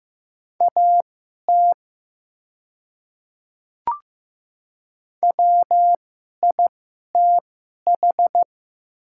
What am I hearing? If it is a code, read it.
Morse code, heard as a keyed tone, about 15 wpm, spaced as for 11 wpm, 700 Hz. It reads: AT WITH
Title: 2x Character Spacing - Top 100 Words - 15wpm